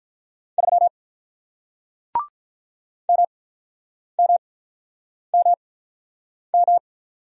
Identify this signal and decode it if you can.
Morse code, heard as a keyed tone, about 47 wpm, spaced as for 9 wpm, 700 Hz. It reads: 3 MMMM